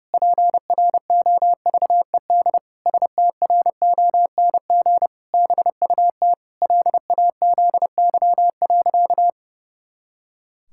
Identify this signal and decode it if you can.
Morse code, heard as a keyed tone, about 32 wpm, 700 Hz. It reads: PROVED STRONG BUT LAZY.
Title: Pangram Challenge v2 - 30wpm